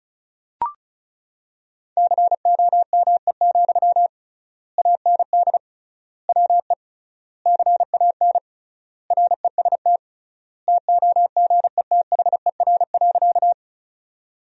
Morse code, heard as a keyed tone, about 36 wpm, 700 Hz. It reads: COME, AND WE CAN REST TOGETHER.
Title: 3x Word-Spacing - Sentences from Top 500 Words - 35wpm